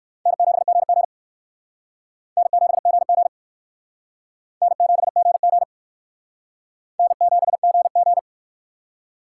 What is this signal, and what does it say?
Morse code, heard as a keyed tone, about 60 wpm, 700 Hz. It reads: N7GG N7GG N7GG N7GG